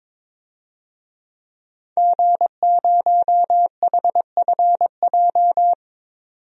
Morse code, heard as a keyed tone, about 22 wpm, 700 Hz. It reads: G0HFJ
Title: DXCC-100 Call Signs - 5 Characters - 22wpm